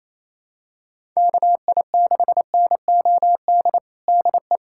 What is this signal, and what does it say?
Morse code, heard as a keyed tone, about 28 wpm, 700 Hz. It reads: KI6NOD DE
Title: USA Callsign DE Callsign - 6 Characters - 28wpm